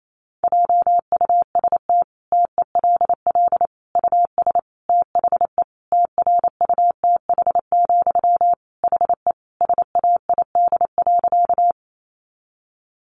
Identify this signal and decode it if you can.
Morse code, heard as a keyed tone, about 28 wpm, 700 Hz. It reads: JUST TELL US THE TRUTH, HE SAID.
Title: Sentences from Top 800 Words 28wpm